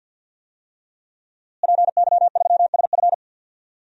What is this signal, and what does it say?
Morse code, heard as a keyed tone, about 51 wpm, 700 Hz. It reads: PY3SF
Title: DXCC-100 Call Signs - 5 Characters - Repeated 2x - 50wpm